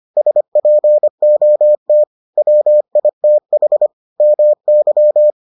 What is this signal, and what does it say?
Morse code, heard as a keyed tone, about 25 wpm, 600 Hz. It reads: SPOT WITH MY